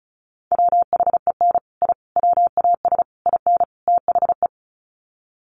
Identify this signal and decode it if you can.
Morse code, heard as a keyed tone, about 34 wpm, 700 Hz. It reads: WHEN I WAS IN THE
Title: Sets of 5 Words Encore2 35wpm